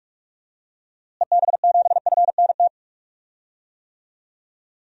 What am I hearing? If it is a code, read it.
Morse code, heard as a keyed tone, about 41 wpm, 700 Hz. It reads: EB7FNT
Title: DXCC-100 Call Signs - 6 Characters - Repeated 2x - 45wpm